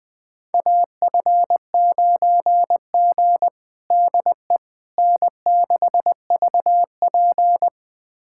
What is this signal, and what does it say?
Morse code, heard as a keyed tone, about 20 wpm, 700 Hz. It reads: AF9G DE N6VP
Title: USA Callsign DE Callsign - 4 Characters - 20wpm